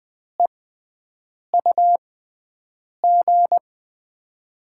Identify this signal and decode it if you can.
Morse code, heard as a keyed tone, about 20 wpm, spaced as for 7 wpm, 700 Hz. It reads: EUG
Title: USA Call Signs - 5 Characters - 6x Character Spacing - 20wpm